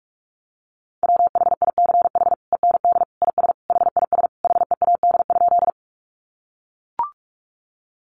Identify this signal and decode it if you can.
Morse code, heard as a keyed tone, about 44 wpm, 700 Hz. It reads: WHICH END IS HIS HEAD?
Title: Sentences from Top 300 Words 45wpm